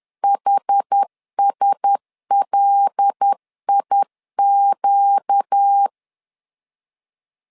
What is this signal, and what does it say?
Morse code, heard as a keyed tone, about 10 wpm, 800 Hz. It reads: HSLIQ
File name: LEZIONI / 5 LBVC / 5 L 10wpm 2259